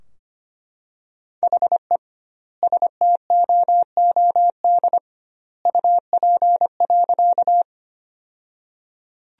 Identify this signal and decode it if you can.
Morse code, heard as a keyed tone, about 25 wpm, 700 Hz. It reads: HE STOOD UP.
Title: 2x Word-Spacing - Sentences from Top 500 Words - 25wpm